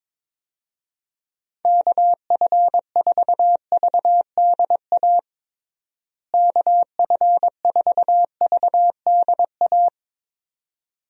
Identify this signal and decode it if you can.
Morse code, heard as a keyed tone, about 22 wpm, 700 Hz. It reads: KF4VDA KF4VDA
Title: USA Call Signs - 6 Characters - Repeated 2x - 22wpm